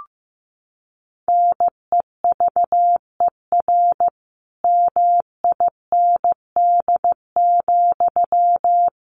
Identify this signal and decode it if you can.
Morse code, heard as a keyed tone, about 14 wpm, 700 Hz. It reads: NEVER MIND,
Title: Sentences from Top 1300 Words 15wpm